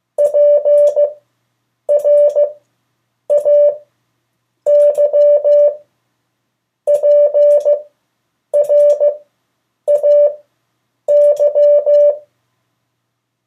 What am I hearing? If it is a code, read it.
Morse code, heard as a keyed tone, about 15 wpm, spaced as for 8 wpm, 580 Hz. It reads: PRAY PRAY
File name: 3of4-words-2